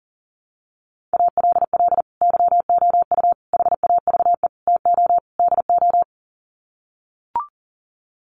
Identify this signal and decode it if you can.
Morse code, heard as a keyed tone, about 37 wpm, 700 Hz. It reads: ALL YOU HAVE TO DO
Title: Sets of 5 Words 40wpm